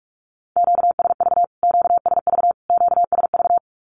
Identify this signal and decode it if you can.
Morse code, heard as a keyed tone, about 45 wpm, 700 Hz. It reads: QSV QSV QSV
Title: Q-Codes Repeated 3x 45wpm